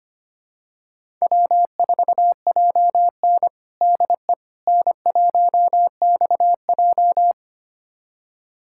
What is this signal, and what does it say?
Morse code, heard as a keyed tone, about 25 wpm, 700 Hz. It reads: W4JN DE N1XJ
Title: USA Callsign DE Callsign - 4 Characters - 25wpm